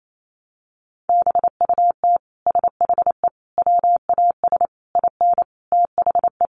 Morse code, heard as a keyed tone, about 28 wpm, 700 Hz. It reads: BUT SHE WAS IN THE
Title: Sentences from Top 1200 Words 28wpm